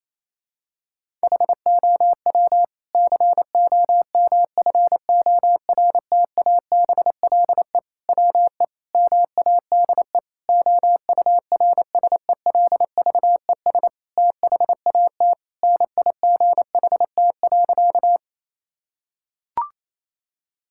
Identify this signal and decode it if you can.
Morse code, heard as a keyed tone, about 28 wpm, 700 Hz. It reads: HOW COMFORTABLE WE MADE OURSELVES THAT NIGHT.
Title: Sentences from Top 1600 Words 28wpm